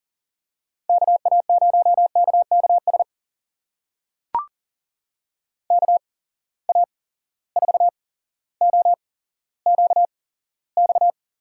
Morse code, heard as a keyed tone, about 41 wpm, spaced as for 17 wpm, 700 Hz. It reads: KA0KKS KA4OQX
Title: USA Call Signs - 8x Character Spacing - 40wpm